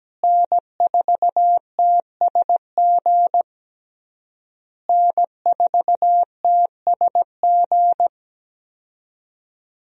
Morse code, heard as a keyed tone, about 17 wpm, 700 Hz. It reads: N4TSG N4TSG
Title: USA Call Signs - 5 Characters - Repeated 3x - 17wpm